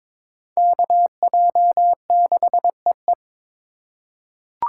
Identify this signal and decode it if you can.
Morse code, heard as a keyed tone, about 22 wpm, 700 Hz. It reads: KJ6EE
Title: USA Call Signs - 2x Character Spacing - 22wpm